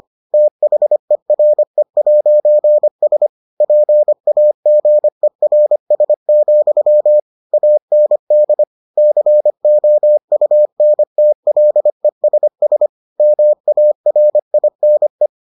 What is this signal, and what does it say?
Morse code, heard as a keyed tone, about 25 wpm, 600 Hz. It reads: THERE'S PAGERS, AND COUNTLESS MARINE